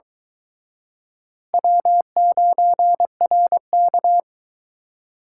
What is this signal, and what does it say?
Morse code, heard as a keyed tone, about 23 wpm, 700 Hz. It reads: W9RK